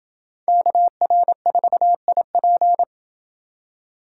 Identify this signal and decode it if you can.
Morse code, heard as a keyed tone, about 28 wpm, 700 Hz. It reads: KR4IP